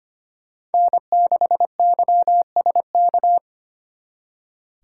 Morse code, heard as a keyed tone, about 33 wpm, 700 Hz. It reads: N6YSK